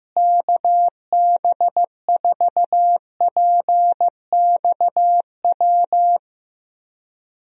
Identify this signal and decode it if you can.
Morse code, heard as a keyed tone, about 15 wpm, 700 Hz. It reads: KB4PXW
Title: USA Call Signs - 6 Characters - Repeated 2x - 15wpm